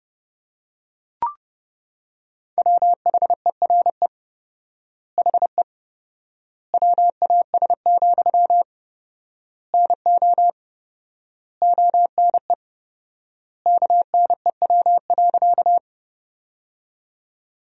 Morse code, heard as a keyed tone, about 30 wpm, 700 Hz. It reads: WHERE HE WAS, NO ONE KNEW.
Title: 4x Word-Spacing - Sentences from Top 400 Words - 30wpm